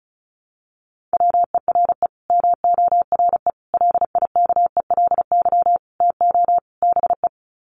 Morse code, heard as a keyed tone, about 34 wpm, 700 Hz. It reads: WERE MORE LIKELY TO BE
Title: Sets of 5 Words Encore2 35wpm